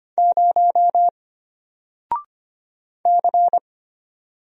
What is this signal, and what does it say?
Morse code, heard as a keyed tone, about 25 wpm, 700 Hz. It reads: 0 C
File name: Single-Letter-Number-25wpm